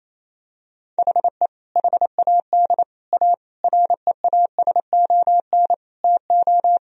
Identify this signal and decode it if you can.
Morse code, heard as a keyed tone, about 28 wpm, 700 Hz. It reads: HE HAD A REASON TO
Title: Sentences from Top 1000 Words - Review Part 1 - 28wpm